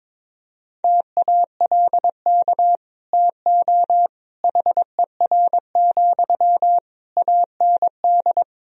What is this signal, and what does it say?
Morse code, heard as a keyed tone, about 21 wpm, 700 Hz. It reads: TALK TO HER, AND
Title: Sentences from Top 1000 Words - Review Part 3 - 22wpm